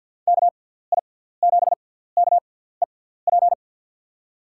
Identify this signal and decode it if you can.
Morse code, heard as a keyed tone, about 47 wpm, spaced as for 17 wpm, 700 Hz. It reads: KI7KEP
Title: USA Call Signs - 6x Character Spacing - 50wpm